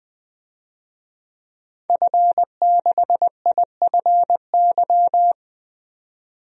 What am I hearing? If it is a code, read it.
Morse code, heard as a keyed tone, about 20 wpm, 700 Hz. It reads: F6IFY
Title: DXCC-100 Call Signs - 5 Characters - 20wpm